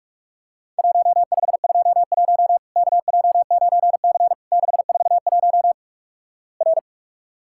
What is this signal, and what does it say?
Morse code, heard as a keyed tone, about 46 wpm, 700 Hz. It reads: 1521 KJ9C 641 R
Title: Worked All Europe DX Contest - QTC Exchange - 45wpm